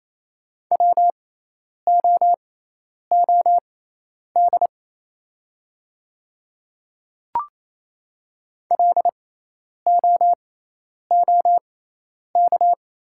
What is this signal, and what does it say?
Morse code, heard as a keyed tone, about 28 wpm, spaced as for 10 wpm, 700 Hz. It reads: WOOD LOOK
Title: 6x Character Spacing - Top 300 Words - 28wpm